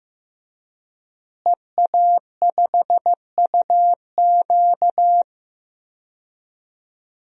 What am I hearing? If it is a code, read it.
Morse code, heard as a keyed tone, about 15 wpm, 700 Hz. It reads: EA5UQ
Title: DXCC-100 Call Signs - 5 Characters - Repeated 2x - 15wpm